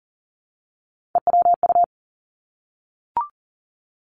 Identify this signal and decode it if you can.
Morse code, heard as a keyed tone, about 40 wpm, 700 Hz. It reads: EWU
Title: Three Challenging Letter-Number Combinations 40wpm